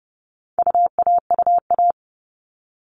Morse code, heard as a keyed tone, about 30 wpm, 700 Hz. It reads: UAUA
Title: All Cut Numbers - 4-Digit 30wpm